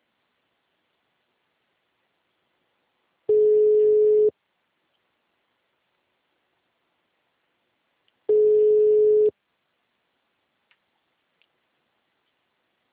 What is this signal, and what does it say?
Ringback tone (Europe, 425 Hz).